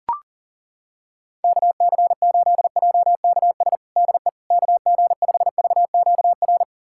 Morse code, heard as a keyed tone, about 40 wpm, 700 Hz. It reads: KC8JKS DE KG5VQR
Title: USA Callsign DE Callsign - 6 Characters - 40wpm